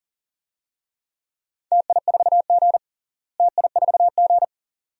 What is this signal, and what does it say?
Morse code, heard as a keyed tone, about 40 wpm, 700 Hz. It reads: TI4G TI4G
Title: DXCC-100 Call Signs - 4 Characters - Repeated 2x - 40wpm